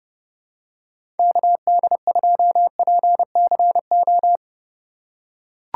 Morse code, heard as a keyed tone, about 30 wpm, 700 Hz. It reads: KD2PCO
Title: USA Call Signs - 6 Characters - 6x Character Spacing - 30wpm